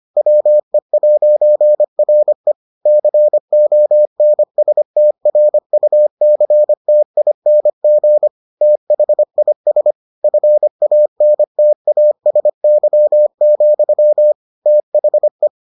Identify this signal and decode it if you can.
Morse code, heard as a keyed tone, about 25 wpm, 600 Hz. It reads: WE'RE CONSTRUCTING THIS FANTASY, THE